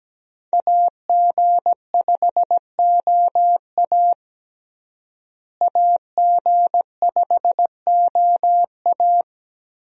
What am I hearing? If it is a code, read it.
Morse code, heard as a keyed tone, about 17 wpm, 700 Hz. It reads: AG5OA AG5OA
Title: USA Call Signs - 5 Characters - Repeated 2x - 17wpm